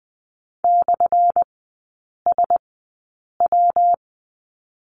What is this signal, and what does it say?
Morse code, heard as a keyed tone, about 20 wpm, spaced as for 8 wpm, 700 Hz. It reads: /SW
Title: 184 Lesson 30e - Letter b - Three Characters 20wpm